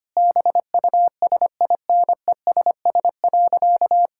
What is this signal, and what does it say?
Morse code, heard as a keyed tone, about 25 wpm, 700 Hz. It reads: BUSINESS.
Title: Sentences from Top 1000 Words - Review Part 6 - 25wpm